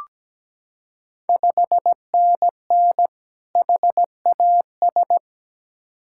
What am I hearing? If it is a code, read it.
Morse code, heard as a keyed tone, about 17 wpm, 700 Hz. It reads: 5NN HAS
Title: Ontario QSO Party Exchange 17wpm